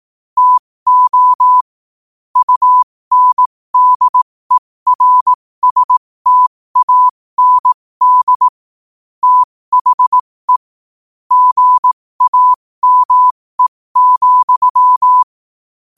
Morse code, heard as a keyed tone, about 18 wpm, spaced as for 15 wpm, 1000 Hz. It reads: TO UNDERSTAND THE GAME,
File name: 15wpm